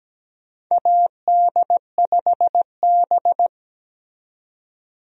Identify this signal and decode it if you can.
Morse code, heard as a keyed tone, about 17 wpm, 700 Hz. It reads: AD5B
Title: USA Call Signs - 4 Characters - 17wpm